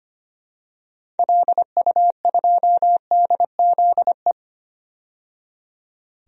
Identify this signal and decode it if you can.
Morse code, heard as a keyed tone, about 25 wpm, 700 Hz. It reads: LU2DZE
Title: DXCC-100 Call Signs - 6 Characters - 25wpm